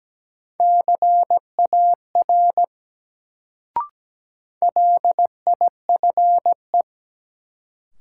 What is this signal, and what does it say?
Morse code, heard as a keyed tone, about 17 wpm, 700 Hz. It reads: CAR LIFE
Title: Sentences from Top 300 Words - New Word Drill - 17wpm